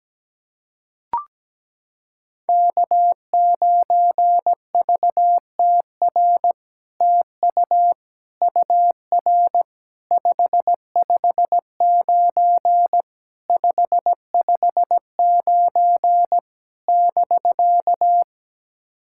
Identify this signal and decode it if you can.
Morse code, heard as a keyed tone, about 17 wpm, 700 Hz. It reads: K9VTR TU UR 559 559 <BK>
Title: SOTA QSO Practice 17wpm